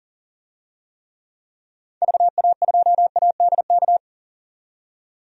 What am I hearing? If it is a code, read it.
Morse code, heard as a keyed tone, about 40 wpm, 700 Hz. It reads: VA2ADK
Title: DXCC-100 Call Signs - 6 Characters - 40wpm